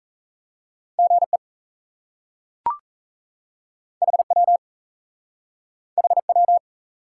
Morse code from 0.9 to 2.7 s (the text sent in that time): GE